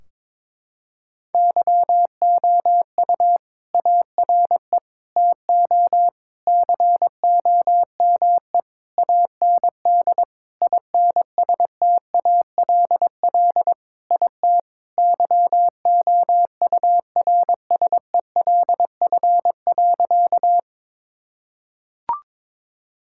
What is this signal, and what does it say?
Morse code, heard as a keyed tone, about 22 wpm, 700 Hz. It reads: YOU ARE TO COME AND INSTALL IT YOURSELF.
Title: Sentences from Top 1600 Words 22wpm